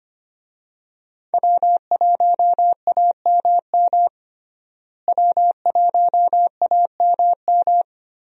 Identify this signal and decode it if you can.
Morse code, heard as a keyed tone, about 25 wpm, 700 Hz. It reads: W1AMM W1AMM
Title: USA Call Signs - 5 Characters - Repeated 3x - 25wpm